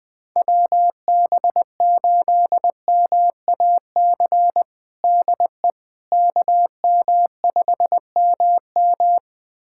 Morse code, heard as a keyed tone, about 20 wpm, 700 Hz. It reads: WB8MAC DE KM5MM
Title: USA Callsign DE Callsign 20wpm